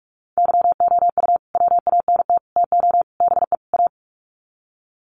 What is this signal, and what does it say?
Morse code, heard as a keyed tone, about 44 wpm, 700 Hz. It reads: YOU WANT TO BE A
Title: Sets of 5 Words Encore2 45wpm